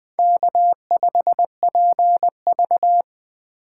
Morse code, heard as a keyed tone, about 19 wpm, 700 Hz. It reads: K5PV